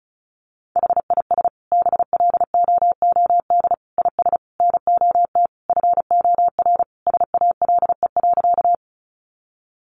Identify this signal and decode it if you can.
Morse code, heard as a keyed tone, about 34 wpm, 700 Hz. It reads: HIS BLOOD IS NOT FOR SALE.
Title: Sentences from Top 1400 Words 35wpm